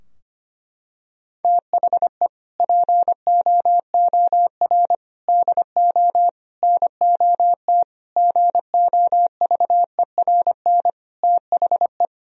Morse code, heard as a keyed tone, about 25 wpm, 700 Hz. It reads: THE POOR DO NOT GOVERN THE
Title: Sentences from Top 1000 Words - Review Part 2 - 25wpm